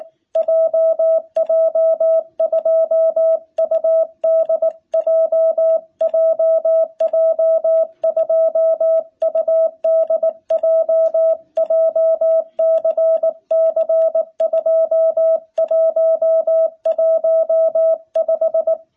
Morse code, heard as a keyed tone, about 19 wpm, 660 Hz. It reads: JJ2UDJ JJ2UDJ JCC2115
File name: jh8cbh / 2021cw